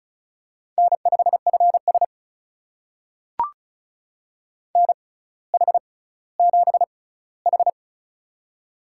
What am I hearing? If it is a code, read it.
Morse code, heard as a keyed tone, about 36 wpm, spaced as for 18 wpm, 700 Hz. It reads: N5FS NH7H
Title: USA Call Signs - 6x Character Spacing - 35wpm